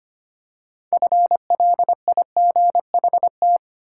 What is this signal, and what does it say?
Morse code, heard as a keyed tone, about 25 wpm, 700 Hz. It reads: FLIGHT
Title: Six Letter Words - 25wpm